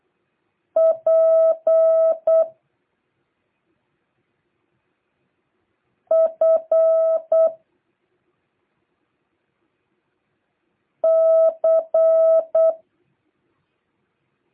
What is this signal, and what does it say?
Morse code, heard as a keyed tone, about 8 wpm, 650 Hz. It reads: P F C